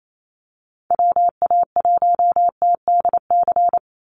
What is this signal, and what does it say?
Morse code, heard as a keyed tone, about 28 wpm, 700 Hz. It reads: WA1TDC
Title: USA Call Signs 28wpm